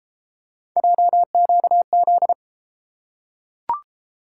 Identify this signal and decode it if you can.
Morse code, heard as a keyed tone, about 31 wpm, 700 Hz. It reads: JQZ